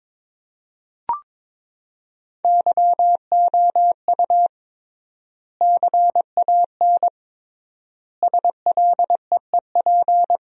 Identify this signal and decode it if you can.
Morse code, heard as a keyed tone, about 22 wpm, 700 Hz. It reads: YOU CAN SLEEP
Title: 3x Word-Spacing - Sentences from Top 500 Words - 22wpm